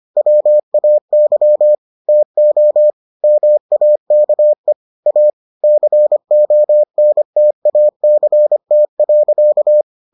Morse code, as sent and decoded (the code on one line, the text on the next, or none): .-- .- -.-- / - --- / -- .- -.- . / .- / -.-. --- -. - .- -.-. - .-.-.-
WAY TO MAKE A CONTACT.